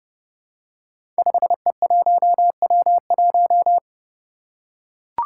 Morse code, heard as a keyed tone, about 30 wpm, 700 Hz. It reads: 5E1W1
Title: Five Challenging Letter-Number Combinations 30wpm